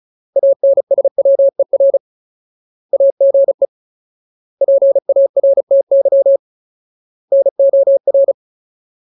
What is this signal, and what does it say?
Morse code, heard as a keyed tone, about 35 wpm, 550 Hz. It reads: ANSWER AGE PARTY NOR
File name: Ultimate Word List / 35WPM Ultimate